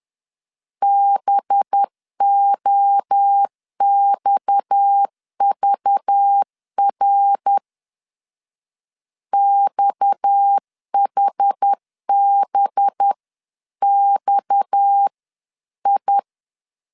Morse code, heard as a keyed tone, about 10 wpm, 800 Hz. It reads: BOXVR XHBXI
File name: LEZIONI / 6 PKX0 / 6 PKX0 10wpm 2707